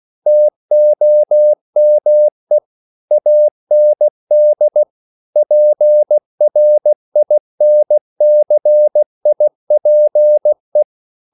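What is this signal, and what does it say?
Morse code, heard as a keyed tone, about 16 wpm, 600 Hz. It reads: TOME AND PRINCIPE
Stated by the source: CW practice of national capitals at 16 words per minute, from ON6ZQ.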